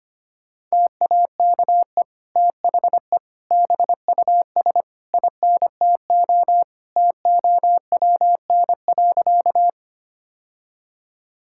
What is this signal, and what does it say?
Morse code, heard as a keyed tone, about 25 wpm, 700 Hz. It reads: TAKE THE BUS INTO TOWN.
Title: Sentences from Top 1000 Words - Review Part 6 - 25wpm